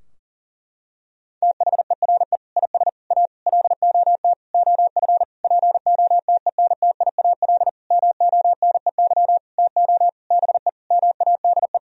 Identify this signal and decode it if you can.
Morse code, heard as a keyed tone, about 40 wpm, 700 Hz. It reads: THERE IS A LOT OF POTENTIAL MONEY TO BE MADE
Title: Sentences from Top 1000 Words - Review Part 4 - 40wpm